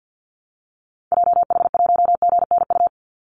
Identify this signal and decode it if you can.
Morse code, heard as a keyed tone, about 50 wpm, 700 Hz. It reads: JH1GNU
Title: Call Signs - 50wpm